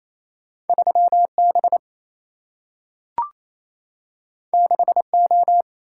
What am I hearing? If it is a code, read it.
Morse code, heard as a keyed tone, about 28 wpm, 700 Hz. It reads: 3B 6O
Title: Single-Letter Single-Number 28wpm